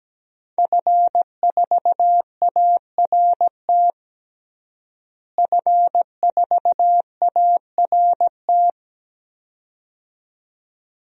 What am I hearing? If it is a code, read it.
Morse code, heard as a keyed tone, about 17 wpm, 700 Hz. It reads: F4ART F4ART
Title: DXCC-100 Call Signs - 5 Characters - Repeated 2x - 17wpm